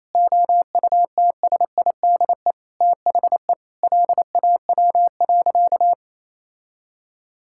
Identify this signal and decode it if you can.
Morse code, heard as a keyed tone, about 27 wpm, 700 Hz. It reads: OUTSIDE THE LAW.
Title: Sentences from Top 1000 Words - Review Part 6 - 28wpm